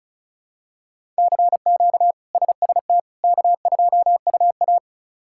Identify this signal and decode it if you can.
Morse code, heard as a keyed tone, about 34 wpm, 700 Hz. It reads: CQ SST K2UA
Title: K1USN SST QSO Practice - 35wpm